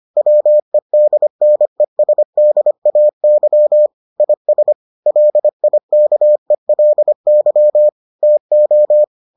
Morse code, heard as a keyed tone, about 25 wpm, 600 Hz. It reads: WEDNESDAY IS LIKELY TO